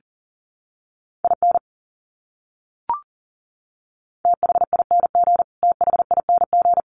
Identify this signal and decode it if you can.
Morse code, heard as a keyed tone, about 40 wpm, 700 Hz. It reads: IN THING THING
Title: Top 200 Words Repeated 40wpm